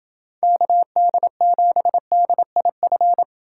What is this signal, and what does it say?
Morse code, heard as a keyed tone, about 27 wpm, 700 Hz. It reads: KD7DIF